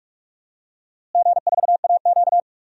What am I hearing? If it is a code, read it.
Morse code, heard as a keyed tone, about 46 wpm, 700 Hz. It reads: G4AQ
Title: DXCC-100 Call Signs - 4 Characters - Repeated 2x - 45wpm